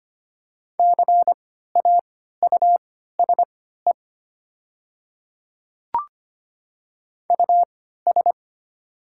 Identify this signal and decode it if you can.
Morse code, heard as a keyed tone, about 25 wpm, spaced as for 14 wpm, 690 Hz. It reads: CAUSE US
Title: 3x Character Spacing - Top 200 Words - 25wpm